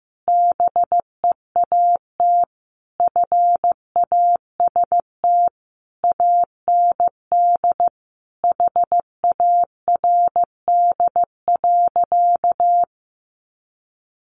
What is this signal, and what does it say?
Morse code, heard as a keyed tone, about 15 wpm, 700 Hz. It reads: BEAT FAST AND HARD.
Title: Sentences from Top 1300 Words 15wpm